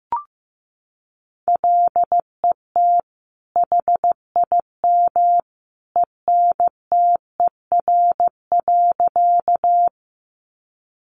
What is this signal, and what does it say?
Morse code, heard as a keyed tone, about 15 wpm, 700 Hz. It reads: LET HIM ENTER.
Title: Sentences from Top 1300 Words 15wpm